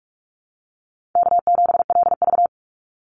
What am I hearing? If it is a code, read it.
Morse code, heard as a keyed tone, about 44 wpm, 700 Hz. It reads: K7LV
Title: Easy Call Signs - 45wpm